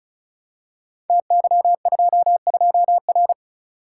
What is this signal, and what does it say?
Morse code, heard as a keyed tone, about 36 wpm, 690 Hz. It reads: TY22R
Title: Code Groups - 5 Letter:Numbers - 35wpm